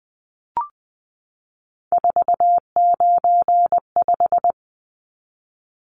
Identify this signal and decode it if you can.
Morse code, heard as a keyed tone, about 19 wpm, 700 Hz. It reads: 495
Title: Numbers - 3-Digit 20wpm